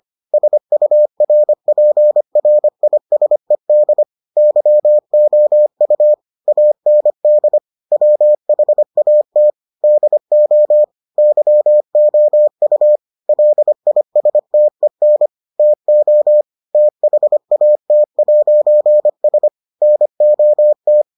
Morse code, heard as a keyed tone, about 25 wpm, 600 Hz. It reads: SURPRISED YOU AND WHAT DO YOU LISTEN TO THAT'S NOT